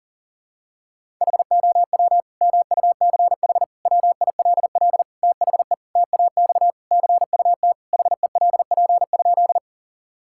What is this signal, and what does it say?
Morse code, heard as a keyed tone, about 37 wpm, 700 Hz. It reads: HOW MUCH WILL THE TAX CUT HELP?
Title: Sentences from Top 1000 Words - Review Part 5 - 40wpm